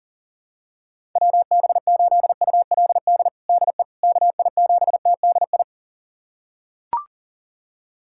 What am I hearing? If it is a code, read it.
Morse code, heard as a keyed tone, about 40 wpm, 700 Hz. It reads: WB8ULD DE KI7TDI
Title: USA Callsign DE Callsign - 6 Characters - 40wpm